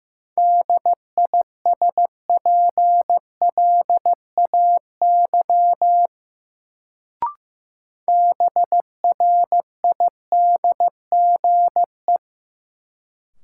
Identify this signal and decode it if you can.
Morse code, heard as a keyed tone, about 15 wpm, 700 Hz. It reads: DISPLAY BRIDGE
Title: Sentences from Top 1200 Words - New Word Drill Alt - 15wpm